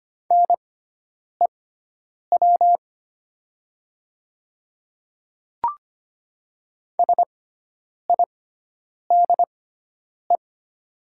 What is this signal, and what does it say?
Morse code, heard as a keyed tone, about 25 wpm, spaced as for 9 wpm, 700 Hz. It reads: NEW SIDE
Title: 6x Character Spacing - Top 500 Words - 25wpm